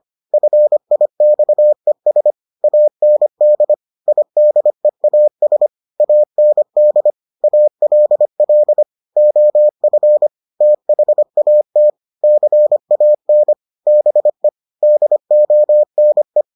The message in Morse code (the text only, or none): FIXES AND IDEAS AND ALL OF THAT CAN BE DONE